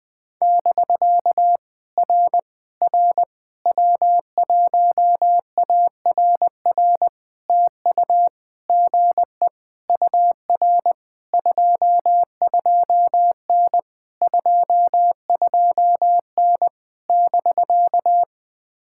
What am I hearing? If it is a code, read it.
Morse code, heard as a keyed tone, about 18 wpm, 700 Hz. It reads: <BK> R R W1ARR TU GE UR 22N 22N <BK>
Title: POTA QSO Practice - 20wpm